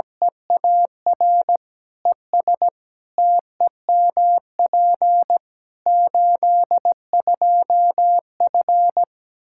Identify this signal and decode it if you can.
Morse code, heard as a keyed tone, about 17 wpm, 700 Hz. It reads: EAR ES TEMP 82F
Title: Practice QSO - Exchange 2 - Info - 17wpm